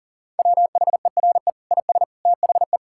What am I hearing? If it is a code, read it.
Morse code, heard as a keyed tone, about 40 wpm, 700 Hz. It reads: WHERE IS THE
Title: Sentences from Top 1000 Words - Review Part 6 - 40wpm